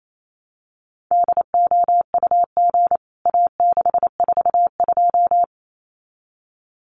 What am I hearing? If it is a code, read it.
Morse code, heard as a keyed tone, about 28 wpm, 700 Hz. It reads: DOUG A642
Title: CWT Exchange 28wpm